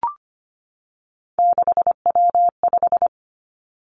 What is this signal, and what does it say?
Morse code, heard as a keyed tone, about 25 wpm, 700 Hz. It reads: 6W5
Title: Three Challenging Letter-Number Combinations 25wpm